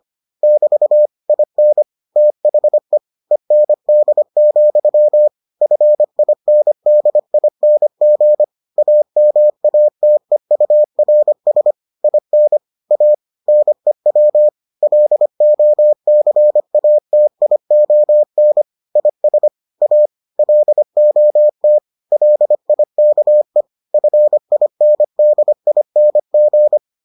Morse code, heard as a keyed tone, about 25 wpm, 600 Hz. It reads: <BT> IN THE END, FINDING AMATEURS IN A NEW LOCATION IS A LOT LIKE FINDING